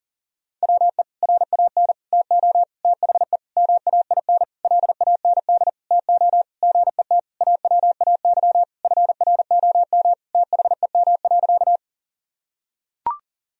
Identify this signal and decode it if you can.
Morse code, heard as a keyed tone, about 40 wpm, 700 Hz. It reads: WE RAN TO THE MAIN LAND TO GET AWAY FROM THEM.
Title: Sentences from Top 1000 Words - Review Part 2 - 40wpm